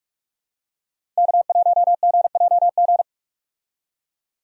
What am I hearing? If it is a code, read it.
Morse code, heard as a keyed tone, about 42 wpm, 700 Hz. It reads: K1GJG
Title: USA Call Signs - 5 Characters - 45wpm